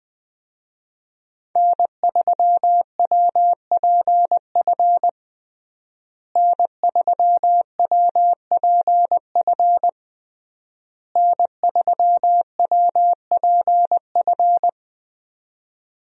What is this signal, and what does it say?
Morse code, heard as a keyed tone, about 20 wpm, 700 Hz. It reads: N3WPF N3WPF N3WPF
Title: USA Call Signs - 5 Characters - Repeated 3x - 20wpm